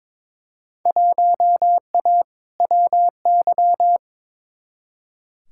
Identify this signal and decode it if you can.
Morse code, heard as a keyed tone, about 22 wpm, 700 Hz. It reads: 1A WY
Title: Field Day Exchanges Encore - 22wpm